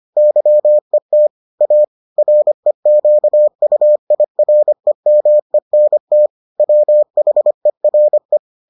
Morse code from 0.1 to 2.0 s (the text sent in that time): YET A